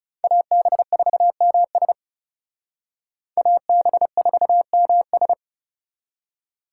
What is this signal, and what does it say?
Morse code, heard as a keyed tone, about 38 wpm, 700 Hz. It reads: AB4MS AB4MS